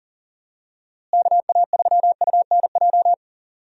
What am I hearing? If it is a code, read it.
Morse code, heard as a keyed tone, about 37 wpm, 700 Hz. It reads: KA3UNJ
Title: USA Call Signs - 6 Characters - Repeated 3x - 40wpm